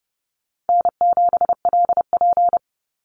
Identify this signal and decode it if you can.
Morse code, heard as a keyed tone, about 30 wpm, 700 Hz. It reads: N7LP
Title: NA SOTA Chaser Callsigns 30wpm